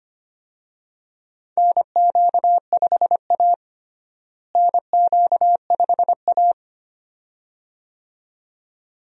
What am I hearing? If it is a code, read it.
Morse code, heard as a keyed tone, about 25 wpm, 700 Hz. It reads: NQ5A NQ5A
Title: USA Call Signs - 4 Characters - Repeated 2x - 25wpm